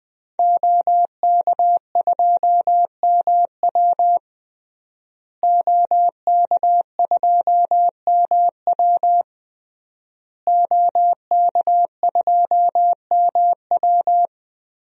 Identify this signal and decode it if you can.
Morse code, heard as a keyed tone, about 20 wpm, 700 Hz. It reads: OK2MW OK2MW OK2MW
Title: DXCC-100 Call Signs - 5 Characters - Repeated 3x - 20wpm